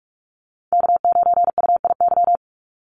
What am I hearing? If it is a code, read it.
Morse code, heard as a keyed tone, about 44 wpm, 700 Hz. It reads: K9UIY
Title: Call Signs Repeated 2x - 45wpm